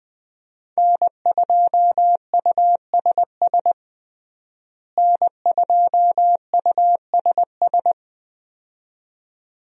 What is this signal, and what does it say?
Morse code, heard as a keyed tone, about 20 wpm, 700 Hz. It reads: N2USS N2USS
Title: USA Call Signs - 5 Characters - Repeated 2x - 20wpm